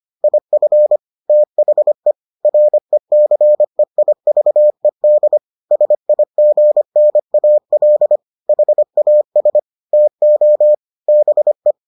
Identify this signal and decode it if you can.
Morse code, heard as a keyed tone, about 25 wpm, 600 Hz. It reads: IF THE RECEIVED SIGNAL HAS TO BE